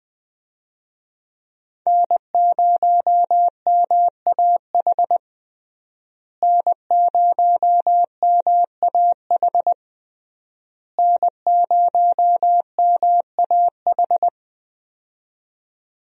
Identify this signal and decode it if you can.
Morse code, heard as a keyed tone, about 20 wpm, 700 Hz. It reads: N0MAH N0MAH N0MAH
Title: USA Call Signs - 5 Characters - Repeated 3x - 20wpm